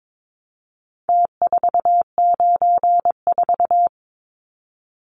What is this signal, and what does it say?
Morse code, heard as a keyed tone, about 22 wpm, 700 Hz. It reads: T494
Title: Numbers - Contest Style - 4-Digit 22wpm